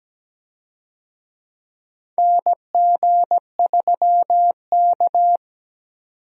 Morse code, heard as a keyed tone, about 17 wpm, 700 Hz. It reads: NG3K
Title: USA Call Signs - 4 Characters - 17wpm